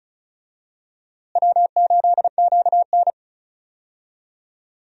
Morse code, heard as a keyed tone, about 36 wpm, 700 Hz. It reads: W8QN
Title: USA Call Signs - 4 Characters - Repeated 3x - 35wpm